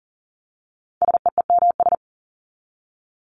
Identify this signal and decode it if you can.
Morse code, heard as a keyed tone, about 40 wpm, 700 Hz. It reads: SEEMS
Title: ES:S Suffix - Top 100 Words with Base Words 40wpm